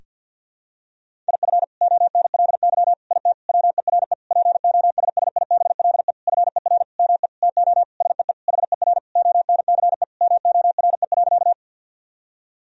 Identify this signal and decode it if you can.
Morse code, heard as a keyed tone, about 52 wpm, 700 Hz. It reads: IF ONLY IT WERE POSSIBLE FER ME TO SEE HER ONCE MORE.
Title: Sentences from Top 500 Words and Abbreviations 50wpm